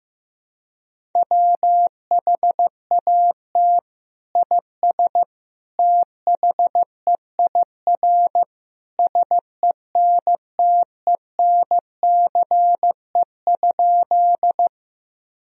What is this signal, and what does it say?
Morse code, heard as a keyed tone, about 15 wpm, 700 Hz. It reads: WHAT IS THEIR SENTENCE?
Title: Sentences from Top 1000 Words - Review Part 1 - 15wpm